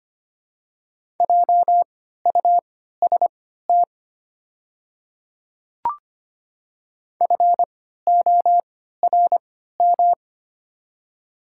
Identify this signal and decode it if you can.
Morse code, heard as a keyed tone, about 25 wpm, spaced as for 14 wpm, 700 Hz. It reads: JUST FORM
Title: 3x Character Spacing - Top 200 Words - 25wpm